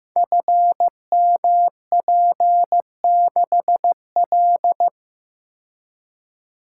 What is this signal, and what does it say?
Morse code, heard as a keyed tone, about 15 wpm, 700 Hz. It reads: FMP6L
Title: Code Groups - 5 Letter:Numbers - 15wpm